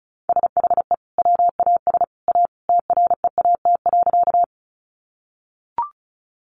Morse code, heard as a keyed tone, about 34 wpm, 700 Hz. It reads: SHE WAS A TREAT.
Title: Sentences from Top 1300 Words 35wpm